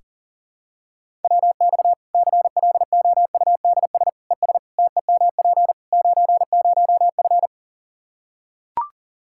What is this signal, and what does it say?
Morse code, heard as a keyed tone, about 40 wpm, 700 Hz. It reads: WX CLOUDS ES TEMP 90F
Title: Practice QSO - Exchange 2 - Info - 40wpm